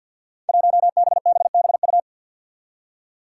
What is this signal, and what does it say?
Morse code, heard as a keyed tone, about 53 wpm, 700 Hz. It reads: 1BBBU
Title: Five Challenging Letter-Number Combinations 50wpm